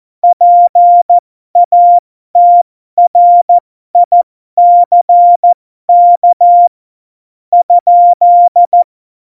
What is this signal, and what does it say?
Morse code, heard as a keyed tone, about 14 wpm, spaced as for 12 wpm, 700 Hz. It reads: PATRICK ?
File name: Morsekurs / Witze